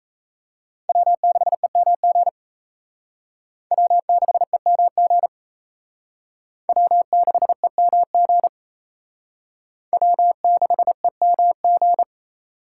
Morse code, heard as a keyed tone, about 37 wpm, 700 Hz. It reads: W6EMG W6EMG W6EMG W6EMG